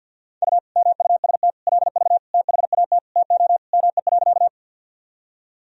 Morse code, heard as a keyed tone, about 49 wpm, 700 Hz. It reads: U MUST LV THAT TO ME.
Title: Sentences from Top 400 Words and Abbreviations 50wpm